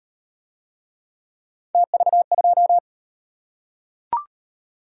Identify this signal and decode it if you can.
Morse code, heard as a keyed tone, about 37 wpm, 700 Hz. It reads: TV2